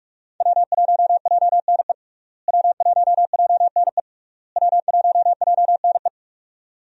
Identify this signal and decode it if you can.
Morse code, heard as a keyed tone, about 46 wpm, 700 Hz. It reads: W1JNE W1JNE W1JNE
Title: USA Call Signs - 5 Characters - Repeated 3x - 45wpm